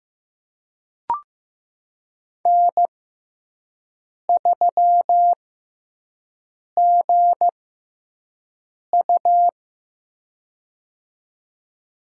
Morse code, heard as a keyed tone, about 15 wpm, spaced as for 5 wpm, 700 Hz. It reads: N3GU
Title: USA Call Signs - 4 Characters - 6x Character Spacing - 15wpm